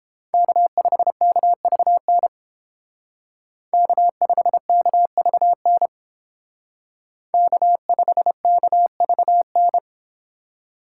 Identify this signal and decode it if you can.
Morse code, heard as a keyed tone, about 28 wpm, 700 Hz. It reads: K5KVN K5KVN K5KVN